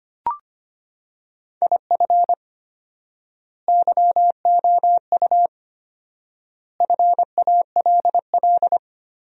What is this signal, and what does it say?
Morse code, heard as a keyed tone, about 25 wpm, 700 Hz. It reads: IF YOU FALL
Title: 4x Word-Spacing - Sentences from Top 500 Words - 25wpm